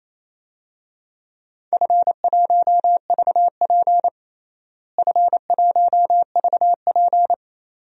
Morse code, heard as a keyed tone, about 28 wpm, 700 Hz. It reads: F1VP F1VP
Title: DXCC-100 Call Signs - 4 Characters - Repeated 2x - 28wpm